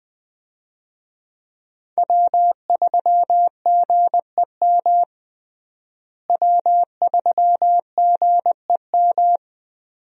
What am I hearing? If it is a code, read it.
Morse code, heard as a keyed tone, about 21 wpm, 700 Hz. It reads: W3GEM W3GEM
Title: USA Call Signs - 5 Characters - Repeated 2x - 20wpm